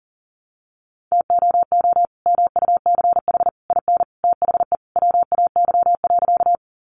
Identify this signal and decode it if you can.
Morse code, heard as a keyed tone, about 40 wpm, 700 Hz. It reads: TOO MUCH IN THE WAY.
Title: Sentences from Top 200 Words 40wpm